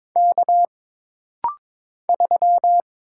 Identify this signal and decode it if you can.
Morse code, heard as a keyed tone, about 22 wpm, 700 Hz. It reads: K 3
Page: This rapid-fire timing is geared to push your ICR (Instant Character Recognition) skills.